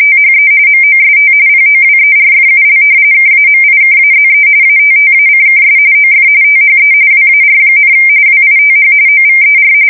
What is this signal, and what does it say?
RTTY, about 45.45 baud. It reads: THE MOST POWERFUL KAMIKAZE ATTACKS OF THE WAR, WITH OVER
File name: fsk170 / battleshipnj